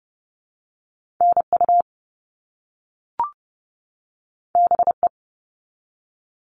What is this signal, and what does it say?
Morse code, heard as a keyed tone, about 30 wpm, 700 Hz. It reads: NU BE
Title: All Cut Numbers - 2-Digit 30wpm